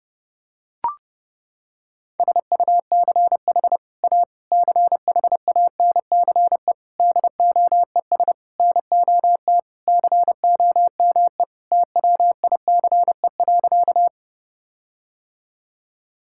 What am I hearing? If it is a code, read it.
Morse code, heard as a keyed tone, about 32 wpm, 700 Hz. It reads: SUCH A CHANCE DOES NOT COME TWICE.
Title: Sentences from Top 1600 Words 30wpm